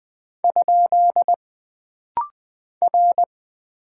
Morse code, heard as a keyed tone, about 20 wpm, 700 Hz. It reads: ? R
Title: Single Character (Letter:Number:Punctuation:Prosign) - Rapid-Fire - 20wpm